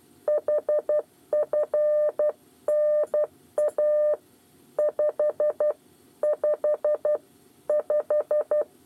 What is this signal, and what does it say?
Morse code, heard as a keyed tone, about 12 wpm, 600 Hz. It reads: HFNA 5 5 5
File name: 8 bis lezione cw 2